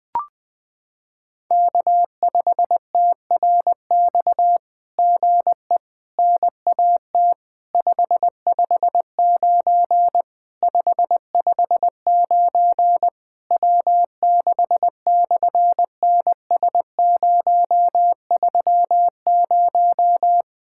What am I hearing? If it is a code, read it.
Morse code, heard as a keyed tone, about 20 wpm, 700 Hz. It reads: K5TRX GE NAT 559 559 W6/NS030
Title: SOTA QSO Practice 20wpm